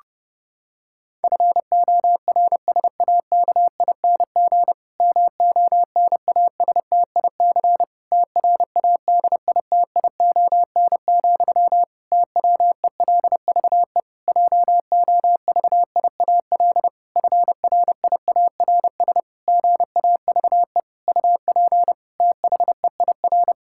Morse code, heard as a keyed tone, about 30 wpm, 700 Hz. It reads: FORSAKING MONASTIC TRADITION, TWELVE JOVIAL FRIARS GAVE UP THEIR
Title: Pangram Challenge v2 - 30wpm